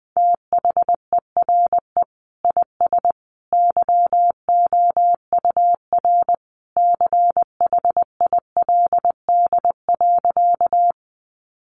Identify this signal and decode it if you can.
Morse code, heard as a keyed tone, about 20 wpm, 700 Hz. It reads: THERE IS YOUR CHILD.
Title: Sentences from Top 800 Words 20wpm